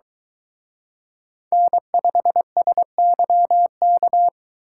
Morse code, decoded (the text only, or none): N5SYK